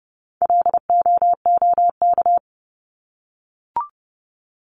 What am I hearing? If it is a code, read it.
Morse code, heard as a keyed tone, about 30 wpm, 700 Hz. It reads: LOOK
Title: ING Suffix - Top 100 Words with Base Words 30wpm